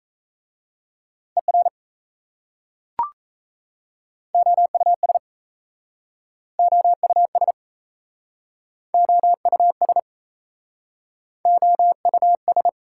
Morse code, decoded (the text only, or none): ER OUS OUS OUS OUS